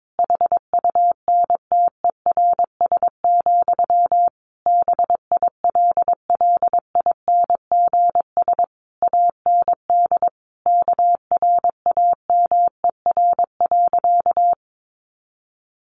Morse code, heard as a keyed tone, about 22 wpm, 700 Hz. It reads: HUNTERS, BILLINGS AND KRAMER.
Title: A Sound Of Thunder 22wpm